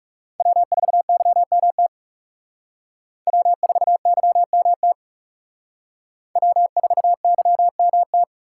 Morse code, heard as a keyed tone, about 38 wpm, 700 Hz. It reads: W4YMT W4YMT W4YMT